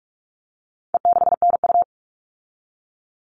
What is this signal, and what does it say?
Morse code, heard as a keyed tone, about 44 wpm, 700 Hz. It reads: E6NU
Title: All Cut Numbers - 4-Digit 45wpm